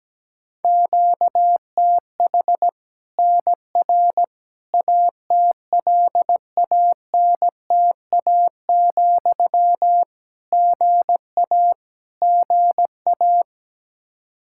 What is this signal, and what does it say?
Morse code, heard as a keyed tone, about 17 wpm, 700 Hz. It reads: QTH NR ATLANTA, GA GA
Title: Practice QSO - Exchange 1 - Beginning - 17wpm